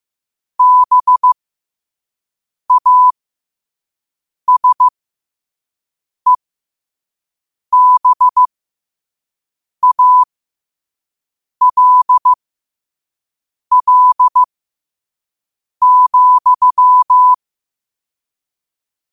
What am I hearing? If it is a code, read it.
Morse code, heard as a keyed tone, about 15 wpm, spaced as for 5 wpm, 1000 Hz. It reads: BASEBALL,
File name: qotdfiles / 5wpm